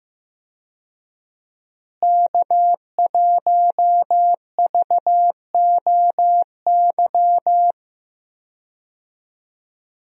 Morse code, heard as a keyed tone, about 15 wpm, 700 Hz. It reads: K1VOY
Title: USA Call Signs - 5 Characters - Repeated 2x - 15wpm